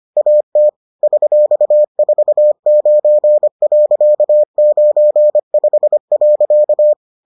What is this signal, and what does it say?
Morse code, heard as a keyed tone, about 25 wpm, 600 Hz. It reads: AT $49.95.